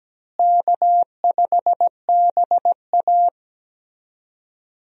Morse code, heard as a keyed tone, about 17 wpm, 700 Hz. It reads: K5BA
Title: USA Call Signs - 4 Characters - Repeated 2x - 17wpm